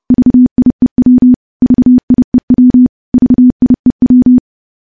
Morse code, heard as a keyed tone, about 30 wpm, 250 Hz. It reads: VIEW VIEW VIEW
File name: morse-training-30WPM